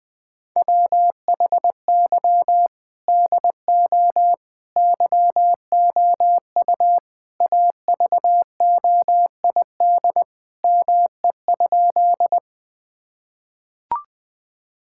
Morse code, heard as a keyed tone, about 20 wpm, 700 Hz. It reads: WHY DO YOU AVOID ME?
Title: Sentences from Top 1000 Words - Review Part 6 - 20wpm